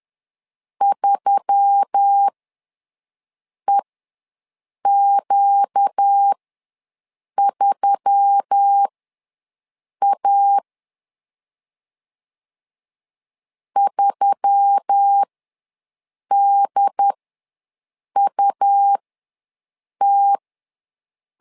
Morse code, heard as a keyed tone, about 10 wpm, spaced as for 6 wpm, 800 Hz. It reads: 3EQ3A 3DUT
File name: LEZIONI / 4 Q1S3 ws 10wpm 2437